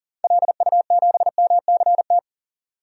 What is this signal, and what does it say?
Morse code, heard as a keyed tone, about 40 wpm, 700 Hz. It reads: LU7MCT
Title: DXCC-100 Call Signs - 6 Characters - 40wpm